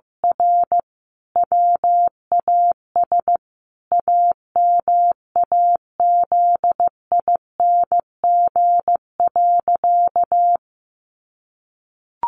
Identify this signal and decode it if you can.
Morse code, heard as a keyed tone, about 15 wpm, 700 Hz. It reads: R WAS AMAZING.
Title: Sentences from Top 1500 Words 15wpm